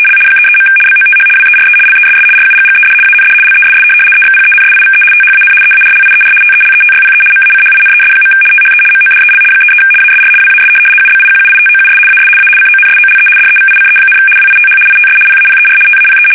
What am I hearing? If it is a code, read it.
RTTY, about 75 baud. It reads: L 33. BOTH WERE LESS RUGGED AND LESS EXPENSIVE THAN EARLIER TELETYPE MACHINES. THE MODEL 33 ASR WAS UBIQUITOUS AS A CONSOLE DEVICE, WITH ITS PAPER TAPE
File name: fsk850-75B / teletype-corp